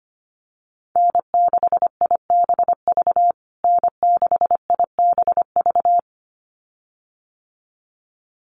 Morse code, heard as a keyed tone, about 25 wpm, 700 Hz. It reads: N6IBV N6IBV
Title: USA Call Signs Repeated 2x 25wpm